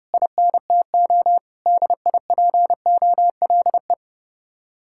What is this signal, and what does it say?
Morse code, heard as a keyed tone, about 37 wpm, 700 Hz. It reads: INTO DIPOLE
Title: Speed-Racing Common QSO Phrases Encore - 30wpm